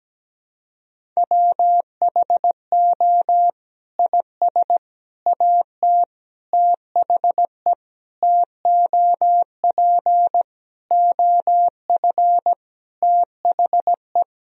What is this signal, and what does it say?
Morse code, heard as a keyed tone, about 17 wpm, 700 Hz. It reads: WHO IS AT THE TOP OF THE
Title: Sentences from Top 1000 Words - Review Part 4 - 17wpm